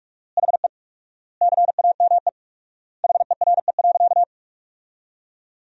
Morse code, heard as a keyed tone, about 44 wpm, 700 Hz. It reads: HE CAME HERE.
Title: 4x Word-Spacing - Sentences from Top 500 Words - 45wpm